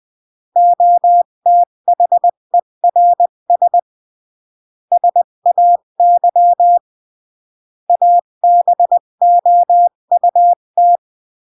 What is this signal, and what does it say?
Morse code, heard as a keyed tone, about 20 wpm, 700 Hz. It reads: OTHERS SAY ABOUT
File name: CW / jodorowsky / 82 commandments 20-15